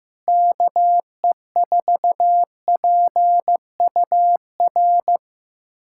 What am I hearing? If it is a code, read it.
Morse code, heard as a keyed tone, about 16 wpm, 700 Hz. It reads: KE4PUR
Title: USA Callsign DE Callsign - 6 Characters - 15wpm